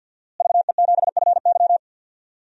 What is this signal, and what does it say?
Morse code, heard as a keyed tone, about 49 wpm, 700 Hz. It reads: VE7FY
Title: DXCC-100 Call Signs - 5 Characters - Repeated 3x - 50wpm